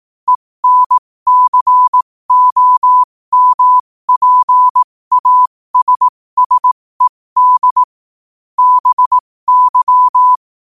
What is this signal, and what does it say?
Morse code, heard as a keyed tone, about 18 wpm, spaced as for 15 wpm, 1000 Hz. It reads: ENCOMPASSED BY
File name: qotdfiles / 15wpm